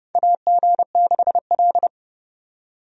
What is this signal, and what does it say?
Morse code, heard as a keyed tone, about 30 wpm, 700 Hz. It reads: AG6L
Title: USA Call Signs - 4 Characters - 30wpm